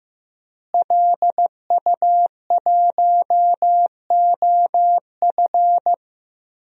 Morse code, heard as a keyed tone, about 15 wpm, 700 Hz. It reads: LU1OF
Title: DXCC-100 Call Signs - 5 Characters - Repeated 3x - 15wpm